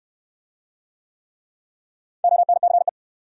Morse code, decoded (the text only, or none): ONCE